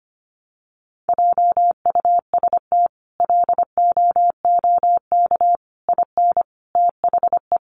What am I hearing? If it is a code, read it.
Morse code, heard as a keyed tone, about 25 wpm, 700 Hz. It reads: JUST LOOK IN THE
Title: Sentences from Top 1300 Words 25wpm